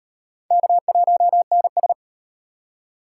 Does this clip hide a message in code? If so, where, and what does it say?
Yes: Morse code from 0.5 to 1.9 s, heard as a keyed tone, about 36 wpm, 700 Hz. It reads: K1NS